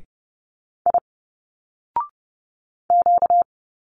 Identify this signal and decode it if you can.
Morse code, heard as a keyed tone, about 30 wpm, 700 Hz. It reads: I Q
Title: Single Letters 30wpm